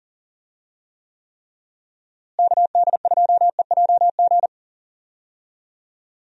Morse code, heard as a keyed tone, about 40 wpm, 700 Hz. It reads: KD2EJG
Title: USA Call Signs - 6 Characters - Repeated 3x - 40wpm